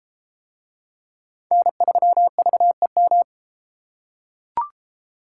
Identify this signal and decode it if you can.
Morse code, heard as a keyed tone, about 32 wpm, 700 Hz. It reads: N3VEM